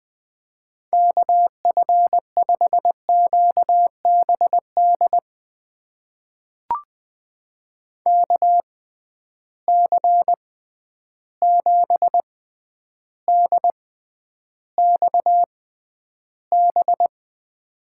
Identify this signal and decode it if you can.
Morse code, heard as a keyed tone, about 20 wpm, spaced as for 10 wpm, 700 Hz. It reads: KF5QBD KC7DXB
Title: USA Call Signs - 6 Characters - 6x Character Spacing - 20wpm